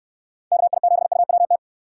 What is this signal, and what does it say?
Morse code, heard as a keyed tone, about 67 wpm, 700 Hz. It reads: KI7UWT